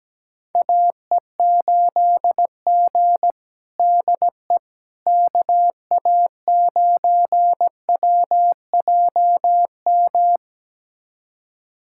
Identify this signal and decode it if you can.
Morse code, heard as a keyed tone, about 17 wpm, 700 Hz. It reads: AE8G DE KA9WJM
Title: USA Callsign DE Callsign 17wpm